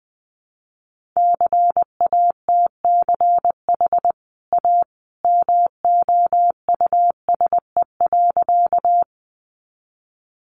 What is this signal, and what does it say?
Morse code, heard as a keyed tone, about 20 wpm, 700 Hz. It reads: CATCH A MOUSE.
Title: Sentences from Top 1500 Words 20wpm